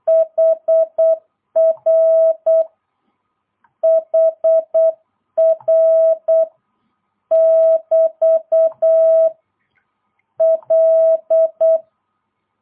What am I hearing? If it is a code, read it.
Morse code, heard as a keyed tone, about 8 wpm, 650 Hz. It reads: HR HR <BT> L
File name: jh8cbh / 2022cw